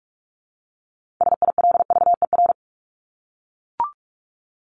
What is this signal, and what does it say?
Morse code, heard as a keyed tone, about 44 wpm, 700 Hz. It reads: SILVER
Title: Top 1751-2000 Words 45wpm